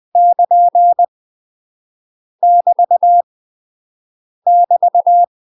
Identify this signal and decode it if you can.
Morse code, heard as a keyed tone, about 20 wpm, 700 Hz. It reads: <KN> <BT> <BT>
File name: UK MASTER QSOs TRAINING / 20WPM / UK MASTER QSOs 0007